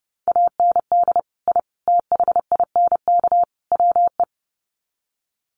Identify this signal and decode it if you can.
Morse code, heard as a keyed tone, about 30 wpm, 700 Hz. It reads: AND I THINK WE
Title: Sets of 4 Words 30wpm